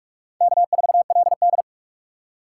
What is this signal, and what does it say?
Morse code, heard as a keyed tone, about 47 wpm, 700 Hz. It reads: K4LD
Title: USA Call Signs - 4 Characters - 4x Character Spacing - 45wpm